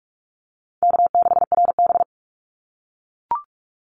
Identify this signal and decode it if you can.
Morse code, heard as a keyed tone, about 44 wpm, 700 Hz. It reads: K6RB
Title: Easy Call Signs - 45wpm